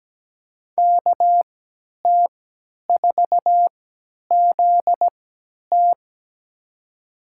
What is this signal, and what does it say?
Morse code, heard as a keyed tone, about 17 wpm, spaced as for 10 wpm, 700 Hz. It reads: KT4ZT
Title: USA Call Signs - 5 Characters - 3x Character Spacing - 17wpm